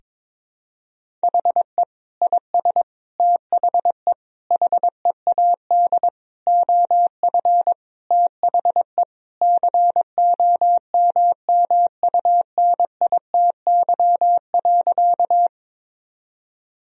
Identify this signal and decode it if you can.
Morse code, heard as a keyed tone, about 22 wpm, 700 Hz. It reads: HE IS THE HEAD OF THE COMMUNITY.
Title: Sentences from Top 1000 Words - Review Part 4 - 22wpm